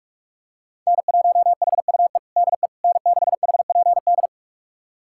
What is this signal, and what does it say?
Morse code, heard as a keyed tone, about 46 wpm, 700 Hz. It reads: N1HUE DE N6HPD
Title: USA Callsign DE Callsign 45wpm